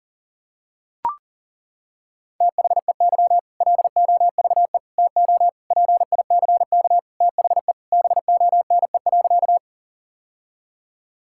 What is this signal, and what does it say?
Morse code, heard as a keyed tone, about 40 wpm, 700 Hz. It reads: THEY LOVE TO PICK THE BONE.
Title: Sentences from Top 1600 Words 40wpm